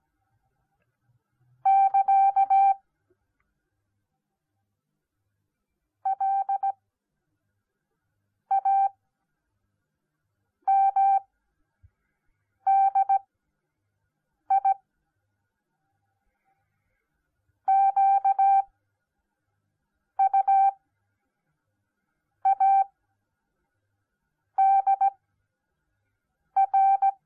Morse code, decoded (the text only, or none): <KA> LAMDI QUADR